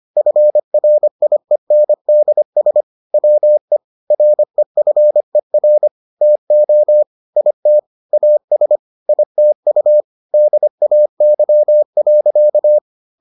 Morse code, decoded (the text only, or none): FRIENDS WE REFER TO IT AS ITU DAY.